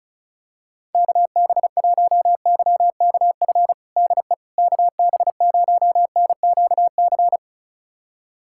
Morse code, heard as a keyed tone, about 36 wpm, 700 Hz. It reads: KB1YKF DE KB0NQC
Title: USA Callsign DE Callsign - 6 Characters - 35wpm